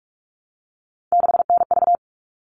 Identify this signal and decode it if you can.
Morse code, heard as a keyed tone, about 44 wpm, 700 Hz. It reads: 6NV
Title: All Cut Numbers - 3-Digit 45wpm